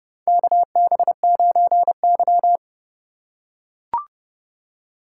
Morse code, heard as a keyed tone, about 31 wpm, 700 Hz. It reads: KB9Y
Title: USA Call Signs - 4 Characters - 8x Character Spacing - 30wpm